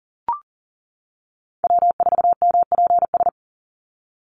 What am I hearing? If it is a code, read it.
Morse code, heard as a keyed tone, about 40 wpm, 700 Hz. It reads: W4MPS
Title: NA SOTA Chaser Callsigns 40wpm